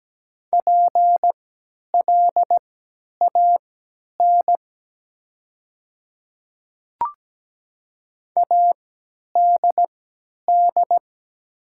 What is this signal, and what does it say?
Morse code, heard as a keyed tone, about 17 wpm, spaced as for 10 wpm, 700 Hz. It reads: PLAN ADD
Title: 3x Character Spacing - Top 500 Words - 17wpm